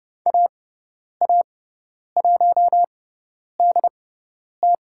Morse code, heard as a keyed tone, about 30 wpm, spaced as for 10 wpm, 700 Hz. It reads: AA1DT
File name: Lesson 17g - Letter u - Call Signs 30wpm